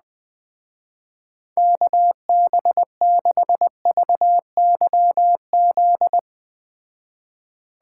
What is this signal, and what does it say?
Morse code, heard as a keyed tone, about 19 wpm, 700 Hz. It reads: KB6VYZ
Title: Speed Racing - USA 2x3 Call Signs 17wpm